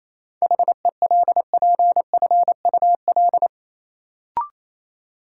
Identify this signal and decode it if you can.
Morse code, heard as a keyed tone, about 28 wpm, 700 Hz. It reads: HELPFUL
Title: Sentences from Top 1000 Words - Review Part 6 - New Word Drill - 28wpm